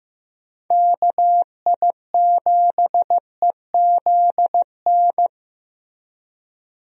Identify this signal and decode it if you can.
Morse code, heard as a keyed tone, about 16 wpm, 700 Hz. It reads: KI7EZN
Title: USA Call Signs - 6 Characters - Repeated 2x - 15wpm